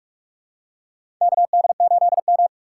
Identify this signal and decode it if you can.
Morse code, heard as a keyed tone, about 46 wpm, 700 Hz. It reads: KD8M
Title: USA Call Signs - 4 Characters - 2x Character Spacing - 45wpm